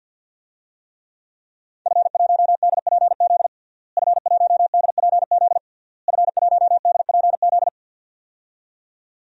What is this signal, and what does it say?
Morse code, heard as a keyed tone, about 49 wpm, 700 Hz. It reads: F1DPZ F1DPZ F1DPZ
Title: DXCC-100 Call Signs - 5 Characters - Repeated 3x - 50wpm